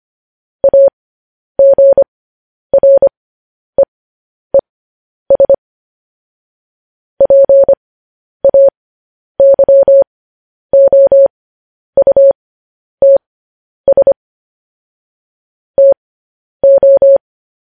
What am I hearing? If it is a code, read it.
Morse code, heard as a keyed tone, about 25 wpm, spaced as for 10 wpm, 550 Hz. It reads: AGREES PAYOUTS TO
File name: podcast / News headlines-1x-25x10